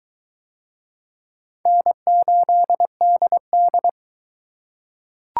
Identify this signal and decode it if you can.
Morse code, heard as a keyed tone, about 20 wpm, 700 Hz. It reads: N8DD